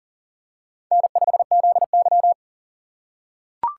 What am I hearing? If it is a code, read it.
Morse code, heard as a keyed tone, about 37 wpm, 700 Hz. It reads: N5ZY
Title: USA Call Signs - 4 Characters - 2x Character Spacing - 40wpm